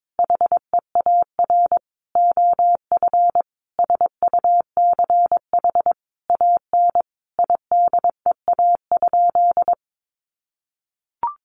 Morse code, heard as a keyed tone, about 22 wpm, 700 Hz. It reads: HEAR OF SUCH AN IDEA?
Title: Sentences from Top 400 Words 22wpm